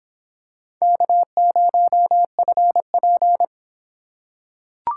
Morse code, heard as a keyed tone, about 26 wpm, 700 Hz. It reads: K0FP